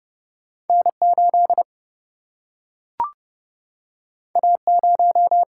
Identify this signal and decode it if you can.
Morse code, heard as a keyed tone, about 30 wpm, 700 Hz. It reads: N 8 A 0
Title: Single-Letter Single-Number 30wpm